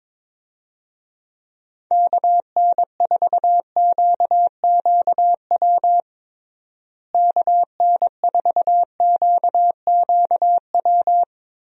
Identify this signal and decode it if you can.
Morse code, heard as a keyed tone, about 23 wpm, 700 Hz. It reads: KN4QQW KN4QQW
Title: USA Call Signs - 6 Characters - Repeated 2x - 22wpm